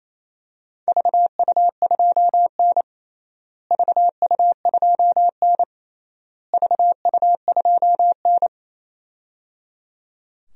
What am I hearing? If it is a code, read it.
Morse code, heard as a keyed tone, about 28 wpm, 700 Hz. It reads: VU2N VU2N VU2N
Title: DXCC-100 Call Signs - 4 Characters - Repeated 3x - 28wpm